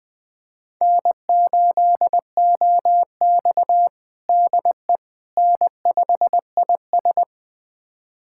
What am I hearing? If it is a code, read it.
Morse code, heard as a keyed tone, about 20 wpm, 700 Hz. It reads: N8OX DE N5IS
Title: USA Callsign DE Callsign - 4 Characters - 20wpm